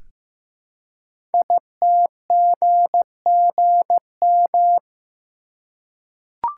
Morse code, heard as a keyed tone, about 15 wpm, 700 Hz. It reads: ITGGM
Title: Code Groups - 5 Letters - 15wpm